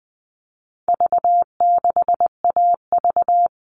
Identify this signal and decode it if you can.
Morse code, heard as a keyed tone, about 20 wpm, 700 Hz. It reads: V6AV
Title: All Cut Numbers - 4-Digit 20wpm